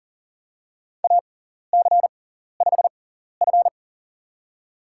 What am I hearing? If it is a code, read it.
Morse code, heard as a keyed tone, about 38 wpm, spaced as for 14 wpm, 700 Hz. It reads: AC5F
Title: USA Call Signs - 4 Characters - 6x Character Spacing - 40wpm